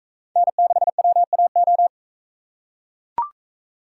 Morse code, heard as a keyed tone, about 38 wpm, 700 Hz. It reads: N6WAO